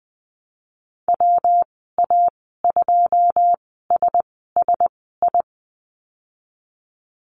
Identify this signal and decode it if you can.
Morse code, heard as a keyed tone, about 20 wpm, spaced as for 15 wpm, 700 Hz. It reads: WA2SSI
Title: 161 Lesson 26g - Letter g - Call Signs 20wpm